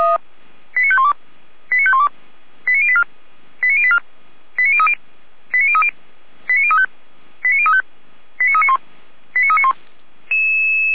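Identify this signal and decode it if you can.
Five-tone selective call (ZVEI), heard as quick runs of five tones, tones of about 80 ms. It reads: 89512 89512 89084 89084 89030 89030 89035 89035 89391 89391
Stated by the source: ZVEI five-tone selective call received by SDR.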